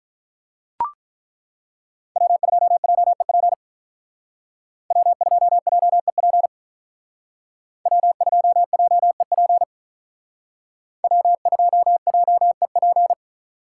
Morse code, heard as a keyed tone, about 42 wpm, 700 Hz. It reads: W2JEP W2JEP W2JEP W2JEP
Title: Speed Racing - USA 1x3 Call Signs 35wpm